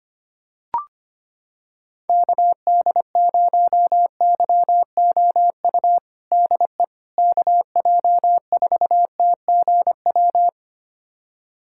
Morse code, heard as a keyed tone, about 25 wpm, 700 Hz. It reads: KD0YOU DE KJ4TGW
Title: USA Callsign DE Callsign - 6 Characters - 25wpm